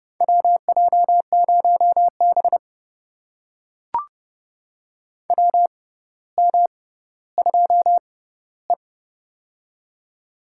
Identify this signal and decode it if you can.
Morse code, heard as a keyed tone, about 30 wpm, spaced as for 15 wpm, 700 Hz. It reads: WJ0B WM2E
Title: USA Call Signs - 4 Characters - 6x Character Spacing - 30wpm